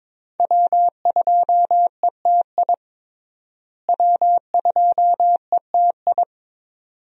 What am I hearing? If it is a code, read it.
Morse code, heard as a keyed tone, about 21 wpm, 700 Hz. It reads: W2ETI W2ETI
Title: USA Call Signs - 5 Characters - Repeated 2x - 22wpm